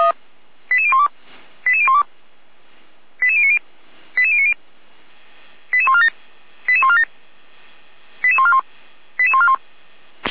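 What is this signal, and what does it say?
Five-tone selective call (ZVEI), heard as quick runs of five tones, tones of about 70 ms. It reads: 88012 88012 88090 88090 88147 88147 88141 88141
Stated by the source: ZVEI five-tone selective call received by SDR, recorded by Feuerwehr Orlamünde.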